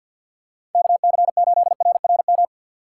Morse code, heard as a keyed tone, about 47 wpm, 700 Hz. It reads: KC8RRM
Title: USA Call Signs - 6 Characters - 3x Character Spacing - 50wpm